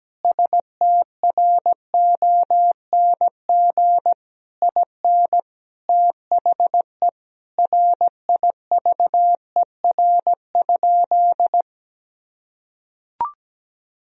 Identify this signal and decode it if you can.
Morse code, heard as a keyed tone, about 17 wpm, 700 Hz. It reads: STRONG IN THE RIVER?
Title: Sentences from Top 1000 Words - Review Part 4 - 17wpm